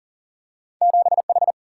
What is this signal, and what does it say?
Morse code, heard as a keyed tone, about 40 wpm, 700 Hz. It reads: 7 H
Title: Single-Letter Single-Number 40wpm